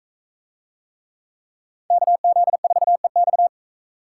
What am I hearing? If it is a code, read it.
Morse code, heard as a keyed tone, about 40 wpm, 700 Hz. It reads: KZ4EX